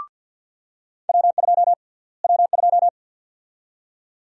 Morse code, heard as a keyed tone, about 46 wpm, 700 Hz. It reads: W2 W2
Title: USA Call Sign Prefixes - Repeated 2x - 50wpm